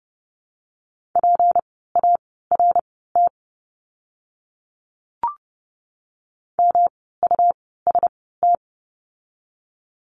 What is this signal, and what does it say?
Morse code, heard as a keyed tone, about 30 wpm, spaced as for 17 wpm, 700 Hz. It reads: PART MUST
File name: course / 30:10wpm / 237 Lesson 38f - Letter q - Words 30wpm